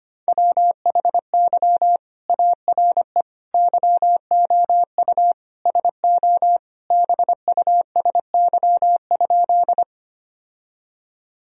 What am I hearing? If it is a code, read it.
Morse code, heard as a keyed tone, about 25 wpm, 700 Hz. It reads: WHY ARE YOU SO BUSY?
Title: Sentences from Top 1000 Words - Review Part 3 - 25wpm